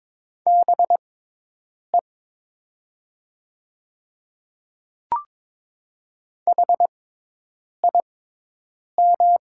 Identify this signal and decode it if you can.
Morse code, heard as a keyed tone, about 22 wpm, spaced as for 8 wpm, 700 Hz. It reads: BE HIM
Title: 6x Character Spacing - Top 100 Words - 22wpm